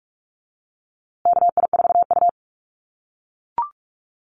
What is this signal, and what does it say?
Morse code, heard as a keyed tone, about 44 wpm, 700 Hz. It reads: KI4U
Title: USA Call Signs Repeated 3x 45wpm